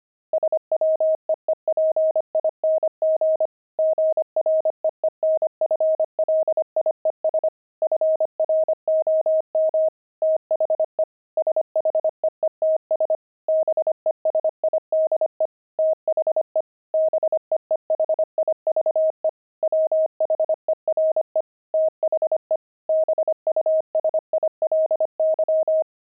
Morse code, heard as a keyed tone, about 25 wpm, 630 Hz. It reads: SWEEPING GREENFLIES FROM THE SHEETS BESIDE THE BEEHIVE WHERE THE BUSILY
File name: pangram-fs-25-cs-25-beehive